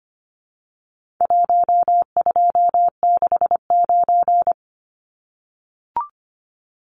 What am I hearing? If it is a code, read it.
Morse code, heard as a keyed tone, about 24 wpm, 700 Hz. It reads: 1269
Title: Numbers - Contest Style - 4-Digit 25wpm